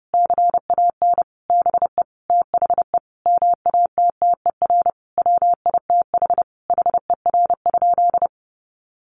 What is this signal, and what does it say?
Morse code, heard as a keyed tone, about 30 wpm, 700 Hz. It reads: CAN BE THE MATTER WITH HER?
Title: Sentences from Top 700 Words 30wpm